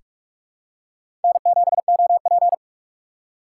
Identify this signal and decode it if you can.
Morse code, heard as a keyed tone, about 46 wpm, 700 Hz. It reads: N7OP
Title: USA Call Signs - 4 Characters - 4x Character Spacing - 45wpm